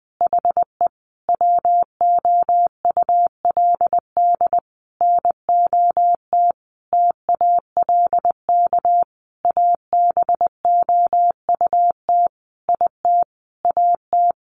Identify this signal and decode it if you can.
Morse code, heard as a keyed tone, about 20 wpm, 700 Hz. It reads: HE WOULD NOT TALK ABOUT IT AT
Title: Sentences from Top 800 Words 20wpm